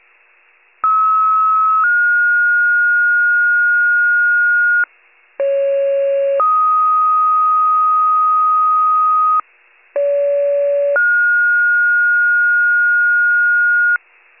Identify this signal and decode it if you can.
Two-tone page, heard as a one-second tone, then a three-second tone, about 1290 then 1430 Hz, about 550 then 1250 Hz, about 570 then 1430 Hz, with noise. It is heard from 0.8 to 14.0 s.